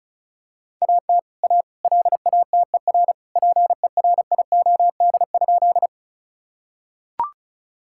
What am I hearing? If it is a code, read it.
Morse code, heard as a keyed tone, about 36 wpm, 700 Hz. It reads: AT A LATER PERIOD?
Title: Sentences from Top 1000 Words - Review Part 5 - 35wpm